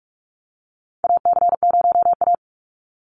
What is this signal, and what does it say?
Morse code, heard as a keyed tone, about 44 wpm, 700 Hz. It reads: AC0A
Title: NA SOTA Chaser Callsigns 45wpm